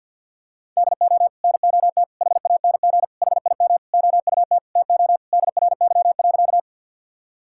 Morse code, heard as a keyed tone, about 50 wpm, 700 Hz. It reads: DO NOT HANG HIM OUT TO DRY.
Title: Sentences from Top 1600 Words 50wpm